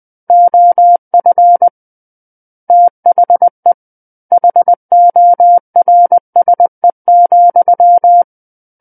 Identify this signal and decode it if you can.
Morse code, heard as a keyed tone, about 20 wpm, 700 Hz. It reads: OF THE HORSE,
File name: 20 WPM